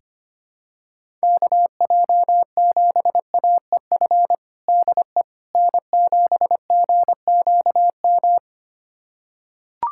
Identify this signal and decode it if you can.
Morse code, heard as a keyed tone, about 25 wpm, 700 Hz. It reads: KJ7AEF DE N7GQM
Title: USA Callsign DE Callsign 25wpm